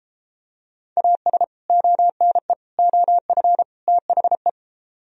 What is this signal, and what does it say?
Morse code, heard as a keyed tone, about 32 wpm, 700 Hz. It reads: AS ONE OF THE